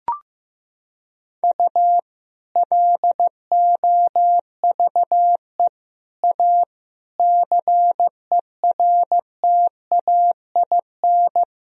Morse code, heard as a keyed tone, about 15 wpm, 700 Hz. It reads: U LOVE A CERTAIN
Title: Sentences from Top 500 Words and Abbreviations 15wpm